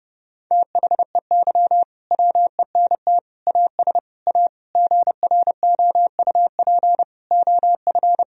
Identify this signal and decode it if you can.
Morse code, heard as a keyed tone, about 30 wpm, 700 Hz. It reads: THEY WENT AS A GROUP OF
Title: Sentences from Top 1000 Words - Review Part 1 - 30wpm